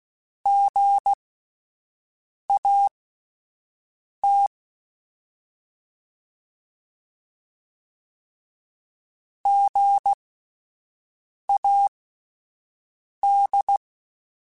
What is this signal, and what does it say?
Morse code, heard as a keyed tone, about 16 wpm, spaced as for 5 wpm, 800 Hz. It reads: GAT GAD
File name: Spaced Alphas / Spaced Alphas mp3s / Gs sp @10